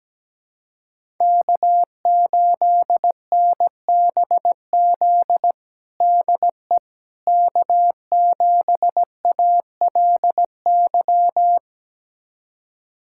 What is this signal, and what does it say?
Morse code, heard as a keyed tone, about 17 wpm, 700 Hz. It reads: K8NBZ DE K7ALY
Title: USA Callsign DE Callsign - 5 Characters - 17wpm